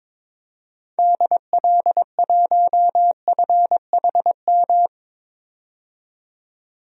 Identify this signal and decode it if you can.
Morse code, heard as a keyed tone, about 22 wpm, 700 Hz. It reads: DL1FHM
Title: DXCC-100 Call Signs - 6 Characters - 22wpm